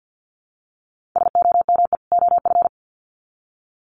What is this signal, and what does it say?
Morse code, heard as a keyed tone, about 50 wpm, 700 Hz. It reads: SOME OF
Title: Sets of 2 Words 50wpm